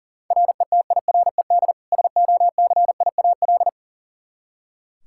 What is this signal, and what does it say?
Morse code, heard as a keyed tone, about 40 wpm, 700 Hz. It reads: RETIRED SOCIAL
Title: Practice QSO - Exchange 3 - More Info - 40wpm